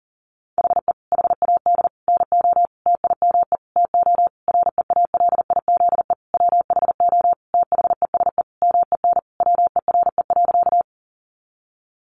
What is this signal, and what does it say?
Morse code, heard as a keyed tone, about 40 wpm, 700 Hz. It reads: HE HAD NO TIME TO REALIZE WHO THESE MEN WERE.
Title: Sentences from Top 1000 Words 40wpm